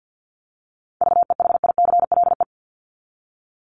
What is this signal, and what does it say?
Morse code, heard as a keyed tone, about 50 wpm, 700 Hz. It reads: VEHICLE
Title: Top 1751-2000 Words 50wpm